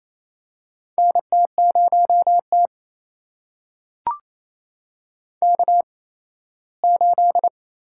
Morse code, heard as a keyed tone, about 27 wpm, spaced as for 17 wpm, 700 Hz. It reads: NT0T K8
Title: USA Call Signs - 4 Characters - 8x Character Spacing - 28wpm